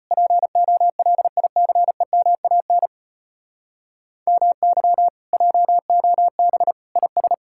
Morse code, heard as a keyed tone, about 33 wpm, 700 Hz. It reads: POLICEMAN MY JOB IS